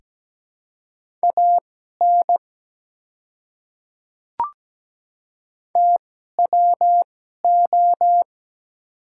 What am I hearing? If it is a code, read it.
Morse code, heard as a keyed tone, about 17 wpm, spaced as for 12 wpm, 700 Hz. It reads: AN TWO
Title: 2x Character Spacing - Top 100 Words - 17wpm